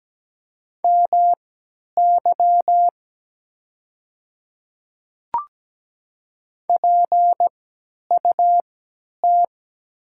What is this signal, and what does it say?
Morse code, heard as a keyed tone, about 17 wpm, spaced as for 10 wpm, 700 Hz. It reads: MY PUT
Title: 3x Character Spacing - Top 300 Words - 17wpm